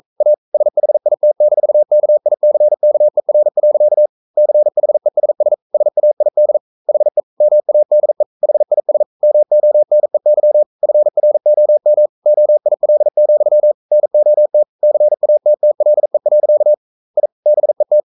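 Morse code, heard as a keyed tone, about 42 wpm, 600 Hz. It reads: A SHIT-KICKER. CHESS SAID HE MADE HIS MONEY FROM OIL, NOT CATTLE. I BET
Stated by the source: Morse code practice at 41 wpm, from CW med Gnister.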